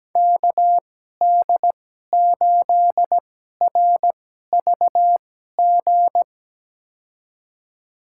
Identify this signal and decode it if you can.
Morse code, heard as a keyed tone, about 17 wpm, spaced as for 12 wpm, 700 Hz. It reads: KD8RVG
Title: USA Call Signs - 6 Characters - 2x Character Spacing - 17wpm